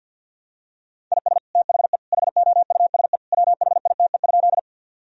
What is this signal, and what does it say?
Morse code, heard as a keyed tone, about 51 wpm, 700 Hz. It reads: IS THE HOUSE WHITE?
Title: Sentences from Top 1000 Words - Review Part 1 - 50wpm